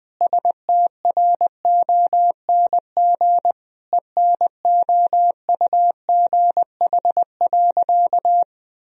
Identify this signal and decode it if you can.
Morse code, heard as a keyed tone, about 21 wpm, 700 Hz. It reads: STRONG ENOUGH.
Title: Sentences from Top 1000 Words - Review Part 3 - 20wpm